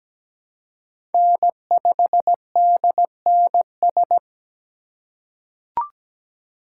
Morse code, heard as a keyed tone, about 17 wpm, 700 Hz. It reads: N5DNS
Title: USA Call Signs - 5 Characters - 8x Character Spacing - 17wpm